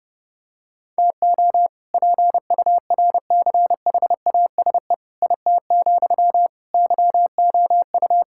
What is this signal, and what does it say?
Morse code, heard as a keyed tone, about 30 wpm, 700 Hz. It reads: TO PURCHASE IT, YOU
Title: Sentences from Top 1000 Words - Review Part 5 - 30wpm